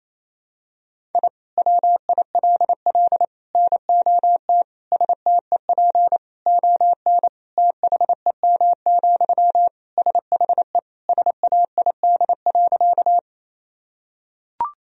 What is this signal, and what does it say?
Morse code, heard as a keyed tone, about 28 wpm, 700 Hz. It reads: I WILL NOT STEP ON THEM, SHE SAID.
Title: Sentences from Top 1000 Words - Review Part 2 - 28wpm